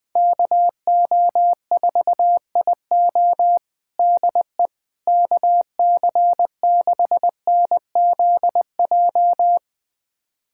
Morse code, heard as a keyed tone, about 21 wpm, 700 Hz. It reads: KO4IO DE KC6NZJ
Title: USA Callsign DE Callsign 20wpm